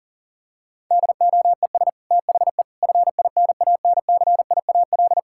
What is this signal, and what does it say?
Morse code, heard as a keyed tone, about 40 wpm, 700 Hz. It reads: DOES THE FINANCIAL
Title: Sentences from Top 1000 Words - Review Part 5 - 40wpm